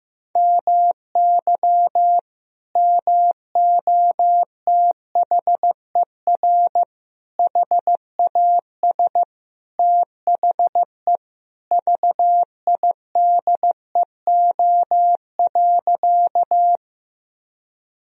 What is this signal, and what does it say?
Morse code, heard as a keyed tone, about 15 wpm, 700 Hz. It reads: MY MOTHER HAS THE VIDEO.
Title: Sentences from Top 1000 Words - Review Part 4 - 15wpm